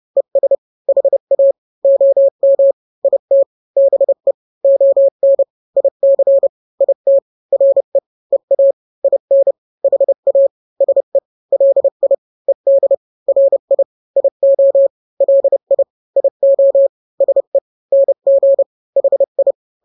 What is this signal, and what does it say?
Morse code, heard as a keyed tone, about 30 wpm, 550 Hz. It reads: ES HA OM IT BE ON IC IT RE EA IN HA SE LI ED RI IO LI IO SE NG HI
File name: n-grams / 2-grams 30 25 WPM